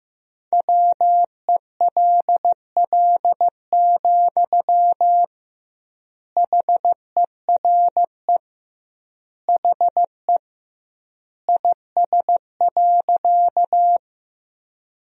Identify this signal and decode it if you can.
Morse code, heard as a keyed tone, about 15 wpm, 700 Hz. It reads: WELL, HERE HE IS.
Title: 2x Word-Spacing - Sentences from Top 200 Words - 15wpm